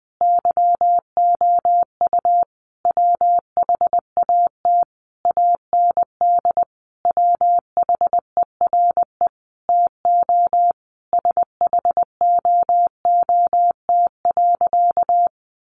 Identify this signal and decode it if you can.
Morse code, heard as a keyed tone, about 20 wpm, 700 Hz. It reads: YOU WHAT AND WHERE TO SHOOT.
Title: A Sound Of Thunder 20wpm